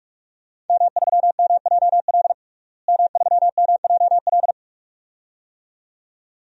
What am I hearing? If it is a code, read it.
Morse code, heard as a keyed tone, about 43 wpm, 700 Hz. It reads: M3MJL M3MJL
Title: DXCC-100 Call Signs - 5 Characters - Repeated 2x - 45wpm